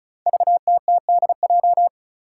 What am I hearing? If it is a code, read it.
Morse code, heard as a keyed tone, about 34 wpm, 700 Hz. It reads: VTTDJ
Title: Five Challenging Letter-Number Combinations 35wpm